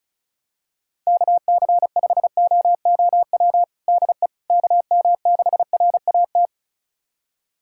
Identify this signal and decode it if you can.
Morse code, heard as a keyed tone, about 36 wpm, 700 Hz. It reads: KC5OOW DE KM6RAT
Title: USA Callsign DE Callsign - 6 Characters - 35wpm